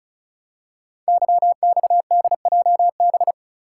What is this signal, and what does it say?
Morse code, heard as a keyed tone, about 36 wpm, 700 Hz. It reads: YXDJB
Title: Code Groups - 5 Letters - 35wpm